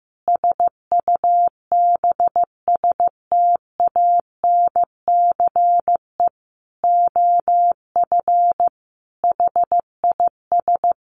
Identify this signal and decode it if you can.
Morse code, heard as a keyed tone, about 15 wpm, 700 Hz. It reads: SUBSTANCE OF HIS
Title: Sentences from Top 1500 Words 15wpm